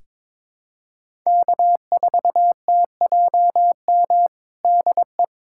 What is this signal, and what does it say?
Morse code, heard as a keyed tone, about 22 wpm, 700 Hz. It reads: K4TJM DE
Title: USA Callsign DE Callsign - 5 Characters - 22wpm